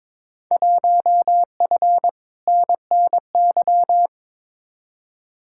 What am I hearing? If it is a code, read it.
Morse code, heard as a keyed tone, about 22 wpm, 700 Hz. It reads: 1F NNY
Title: Field Day Exchanges - 22wpm